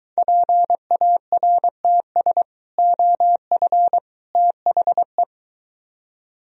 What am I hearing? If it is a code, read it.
Morse code, heard as a keyed tone, about 23 wpm, 700 Hz. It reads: PARTS OF THE